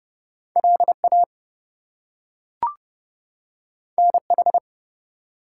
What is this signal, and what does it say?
Morse code, heard as a keyed tone, about 30 wpm, 700 Hz. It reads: LA NH
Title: US States and Canadian Provinces 30wpm